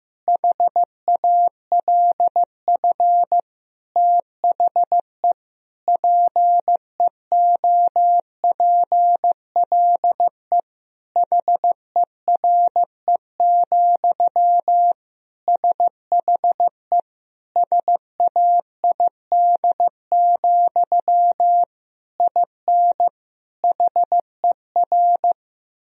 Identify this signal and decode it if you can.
Morse code, heard as a keyed tone, about 15 wpm, 700 Hz. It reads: HALF THE PEOPLE HERE, SHE SAID, IN HER
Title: Sentences from Top 1000 Words - Review Part 1 - 15wpm